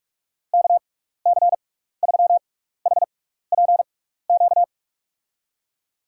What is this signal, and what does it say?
Morse code, heard as a keyed tone, about 41 wpm, spaced as for 15 wpm, 700 Hz. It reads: KC3HPQ
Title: USA Call Signs - 6 Characters - 6x Character Spacing - 45wpm